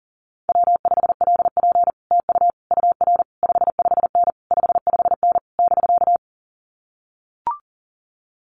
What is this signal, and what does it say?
Morse code, heard as a keyed tone, about 40 wpm, 700 Hz. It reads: W5LP TU UR 55N 55N <BK>
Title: Common QSO Phrases Encore 40wpm